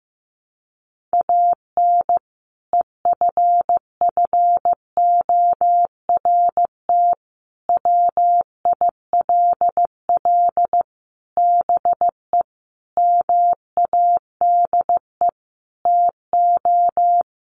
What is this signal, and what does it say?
Morse code, heard as a keyed tone, about 15 wpm, 700 Hz. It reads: AN EFFORT WILL BE MADE TO
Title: Sentences from Top 1500 Words 15wpm